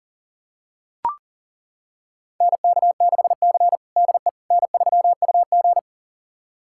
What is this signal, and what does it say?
Morse code, heard as a keyed tone, about 40 wpm, 700 Hz. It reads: NK6C DE N3UG
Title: USA Callsign DE Callsign - 4 Characters - 40wpm